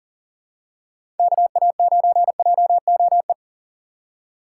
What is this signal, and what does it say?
Morse code, heard as a keyed tone, about 40 wpm, 690 Hz. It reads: KA9JOE
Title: USA Call Signs - 4x Character Spacing - 40wpm